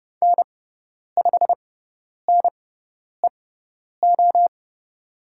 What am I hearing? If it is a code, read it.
Morse code, heard as a keyed tone, about 30 wpm, spaced as for 10 wpm, 700 Hz. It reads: N5NEO
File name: Lesson 16g - Letter c - Call Signs 30wpm